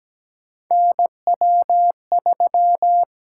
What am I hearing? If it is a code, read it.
Morse code, heard as a keyed tone, about 17 wpm, 700 Hz. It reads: NW3
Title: USA Call Signs - 8x Character Spacing - 17wpm